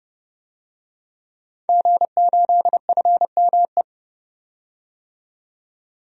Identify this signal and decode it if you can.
Morse code, heard as a keyed tone, about 30 wpm, 700 Hz. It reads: G8FME
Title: DXCC-100 Call Signs - 5 Characters - 30wpm